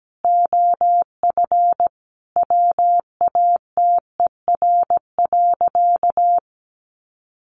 Morse code, heard as a keyed tone, about 17 wpm, 700 Hz. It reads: OF WATER.
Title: Sentences from Top 1300 Words 17wpm